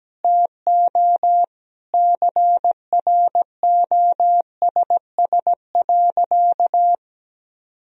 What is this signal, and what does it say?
Morse code, heard as a keyed tone, about 17 wpm, 700 Hz. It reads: TO CROSS.
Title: Sentences from Top 1000 Words - Review Part 1 - 17wpm